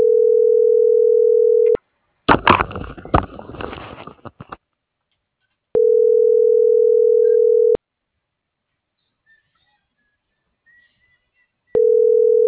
Ringback tone (North America), 440 + 480 Hz.